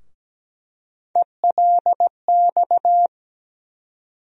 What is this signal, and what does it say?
Morse code, heard as a keyed tone, about 17 wpm, 700 Hz. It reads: ELX
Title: USA Call Sign Postfixes - 3 Letters 17wpm